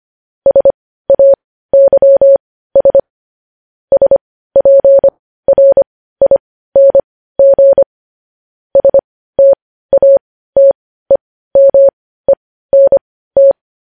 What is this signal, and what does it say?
Morse code, heard as a keyed tone, about 25 wpm, spaced as for 15 wpm, 550 Hz. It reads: SAYS SPRING STATEMENT
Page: news headlines in morse code at 15 wpm